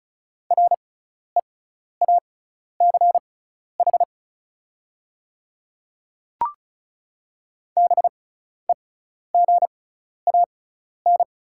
Morse code, heard as a keyed tone, about 36 wpm, spaced as for 12 wpm, 700 Hz. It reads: REACH BEGAN
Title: 6x Character Spacing - Top 500 Words - 35wpm